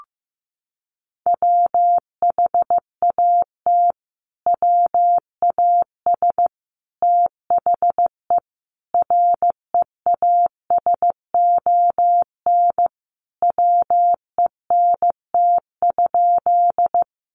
Morse code, heard as a keyed tone, about 15 wpm, 700 Hz. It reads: WHAT WAS THE REASON WENT?
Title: Sentences from Top 700 Words 15wpm